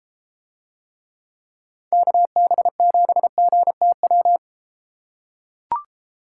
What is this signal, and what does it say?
Morse code, heard as a keyed tone, about 31 wpm, 700 Hz. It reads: KB7GTW